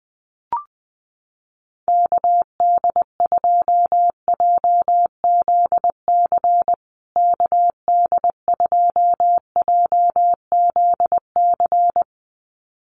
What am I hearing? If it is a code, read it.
Morse code, heard as a keyed tone, about 20 wpm, 700 Hz. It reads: KD2JZC KD2JZC
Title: USA Call Signs Repeated 2x 20wpm